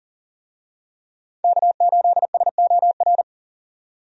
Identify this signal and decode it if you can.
Morse code, heard as a keyed tone, about 40 wpm, 700 Hz. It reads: K8SOR
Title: USA Call Signs - 5 Characters - 40wpm